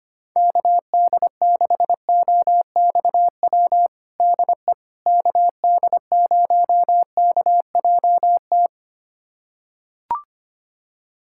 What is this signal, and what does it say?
Morse code, heard as a keyed tone, about 25 wpm, 700 Hz. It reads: KD6OXW DE KD0KJT
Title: USA Callsign DE Callsign - 6 Characters - 25wpm